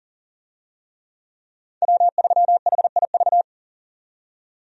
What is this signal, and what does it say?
Morse code, heard as a keyed tone, about 40 wpm, 700 Hz. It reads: W3HIV
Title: USA Call Signs - 5 Characters - 40wpm